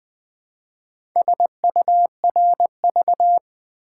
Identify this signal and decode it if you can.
Morse code, heard as a keyed tone, about 20 wpm, 700 Hz. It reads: SURV
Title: Build A Word - 4 Letters At A Time - 8-Letter Words - 20wpm